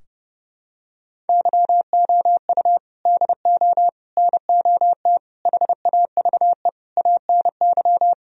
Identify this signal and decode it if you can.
Morse code, heard as a keyed tone, about 30 wpm, 700 Hz. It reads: YOU DO NOT HAVE ANY
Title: Sentences from Top 1000 Words - Review Part 4 - 30wpm